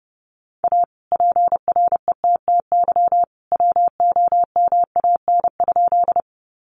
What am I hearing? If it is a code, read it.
Morse code, heard as a keyed tone, about 30 wpm, 700 Hz. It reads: A PRETTY WOMAN?
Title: Sentences from Top 1100 Words 30wpm